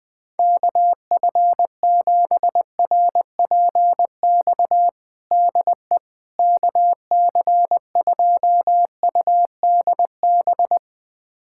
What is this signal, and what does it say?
Morse code, heard as a keyed tone, about 20 wpm, 700 Hz. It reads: KF7RPX DE KC2UDB
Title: USA Callsign DE Callsign - 6 Characters - 20wpm